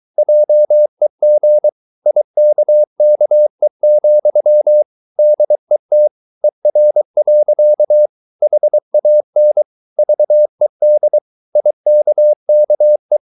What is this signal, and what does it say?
Morse code, heard as a keyed tone, about 23 wpm, 600 Hz. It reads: JEG IKKE, DET ER. HAN VED IKKE
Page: Ferie 23 wpm